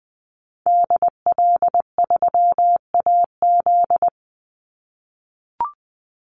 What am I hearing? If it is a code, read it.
Morse code, heard as a keyed tone, about 20 wpm, 700 Hz. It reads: DL3AZ
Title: Call Signs - 20wpm